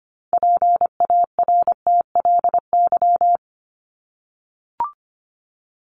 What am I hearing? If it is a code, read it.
Morse code, heard as a keyed tone, about 25 wpm, 700 Hz. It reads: PARTLY
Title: LY Suffix - Top 100 Words with Base Words 25wpm